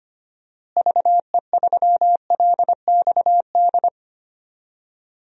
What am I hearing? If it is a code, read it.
Morse code, heard as a keyed tone, about 25 wpm, 700 Hz. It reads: VE3LXD
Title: DXCC-100 Call Signs - 6 Characters - Repeated 3x - 25wpm